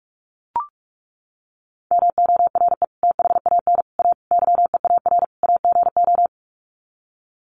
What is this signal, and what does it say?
Morse code, heard as a keyed tone, about 44 wpm, 700 Hz. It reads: MORE THAN A YEAR AGO
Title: Sets of 5 Words Encore2 45wpm